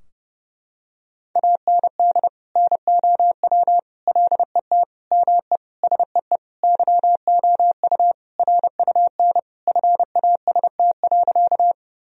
Morse code, heard as a keyed tone, about 32 wpm, 700 Hz. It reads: AND NOW LET ME SEE YOU RUN FAST.
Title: Sentences from Top 1000 Words - Review Part 2 - 30wpm